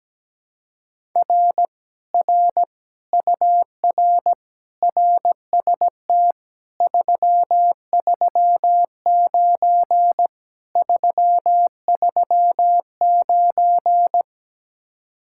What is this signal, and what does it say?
Morse code, heard as a keyed tone, about 17 wpm, 700 Hz. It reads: R R UR RST 339 339
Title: Speed-Racing Common QSO Phrases - 17wpm